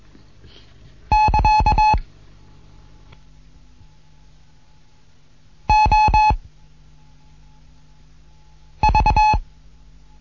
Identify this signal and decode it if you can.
Morse code, heard as a keyed tone, about 21 wpm, 860 Hz. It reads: <KA> O V
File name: Morsen / Kurs A 15